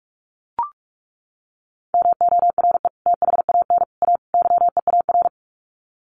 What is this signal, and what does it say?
Morse code, heard as a keyed tone, about 44 wpm, 700 Hz. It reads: MORE THAN A YEAR
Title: Sets of 4 Words 45wpm